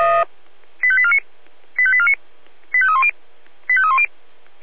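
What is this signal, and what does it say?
Five-tone selective call (ZVEI), heard as quick runs of five tones, tones of about 70 ms. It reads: 85849 85849 85310 85310
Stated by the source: ZVEI five-tone selective call 85840 received by SDR.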